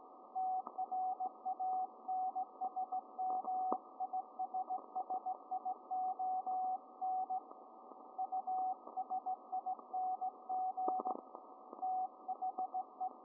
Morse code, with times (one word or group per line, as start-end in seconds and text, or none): 0.4-7.4 TRANSMISSION
8.2-11.1 USING
11.8-13.1 THE